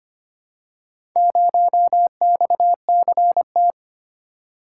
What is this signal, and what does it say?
Morse code, heard as a keyed tone, about 25 wpm, 700 Hz. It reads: 0XCT
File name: practice / 25wpm / Four Letter-Number Combinations - 25wpm